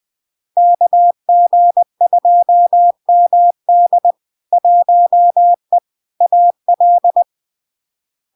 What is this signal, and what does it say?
Morse code, heard as a keyed tone, about 20 wpm, 700 Hz. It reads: KG2MD 1E AL
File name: LARK-FD-2022-Log-in-CW-20wpm-700-Hz